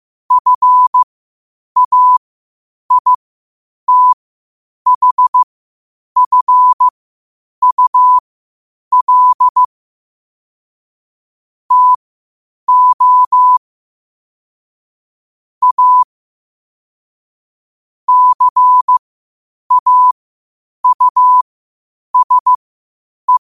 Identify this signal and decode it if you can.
Morse code, heard as a keyed tone, about 15 wpm, spaced as for 8 wpm, 1000 Hz. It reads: FAITHFUL TO A CAUSE
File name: qotdfiles / 8wpm